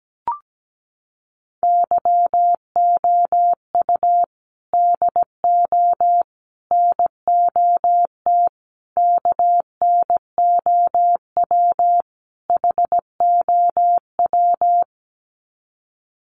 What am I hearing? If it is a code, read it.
Morse code, heard as a keyed tone, about 17 wpm, 700 Hz. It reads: YOU DO NOT KNOW HOW
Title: Sets of 5 Words Encore 17wpm